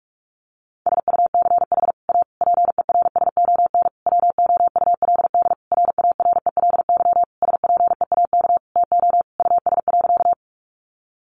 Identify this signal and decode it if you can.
Morse code, heard as a keyed tone, about 44 wpm, 700 Hz. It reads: SUCH A PERSON WOULD RARELY SPEAK TO US.
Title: Sentences from Top 1400 Words 45wpm